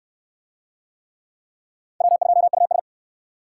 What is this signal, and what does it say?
Morse code, heard as a keyed tone, about 57 wpm, 700 Hz. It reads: W2US